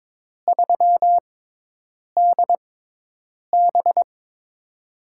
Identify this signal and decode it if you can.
Morse code, heard as a keyed tone, about 22 wpm, spaced as for 8 wpm, 700 Hz. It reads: 3DB
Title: USA Call Signs - 4 Characters - 6x Character Spacing - 22wpm